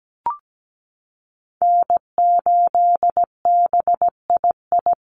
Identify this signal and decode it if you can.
Morse code, heard as a keyed tone, about 17 wpm, 700 Hz. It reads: N8BII
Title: USA Call Signs Repeated 2x 17wpm